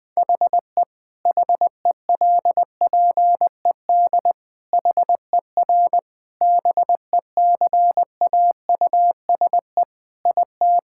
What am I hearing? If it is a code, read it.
Morse code, heard as a keyed tone, about 21 wpm, 700 Hz. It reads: HE HELPED HER BECAUSE IT
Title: Sentences with 8-Word Phrases - Spoken Beginning - 20wpm